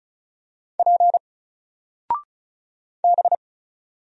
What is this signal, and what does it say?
Morse code, heard as a keyed tone, about 36 wpm, 700 Hz. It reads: P B
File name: Single-Letter-Number-35wpm